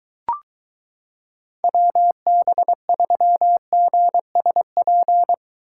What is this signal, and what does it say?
Morse code, heard as a keyed tone, about 23 wpm, 700 Hz. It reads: WB3GSP